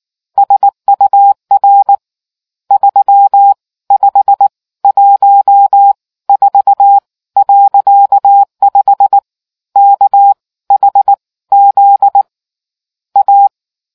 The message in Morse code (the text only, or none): SUR 3514.5 KHZ A